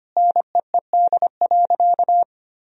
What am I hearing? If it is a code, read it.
Morse code, heard as a keyed tone, about 25 wpm, 700 Hz. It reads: NEED.
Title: Sentences from Top 1000 Words - Review Part 1 - 25wpm